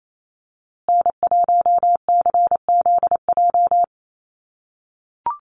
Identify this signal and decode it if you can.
Morse code, heard as a keyed tone, about 28 wpm, 700 Hz. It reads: N1CZJ
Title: USA Call Signs Repeated 3x 28wpm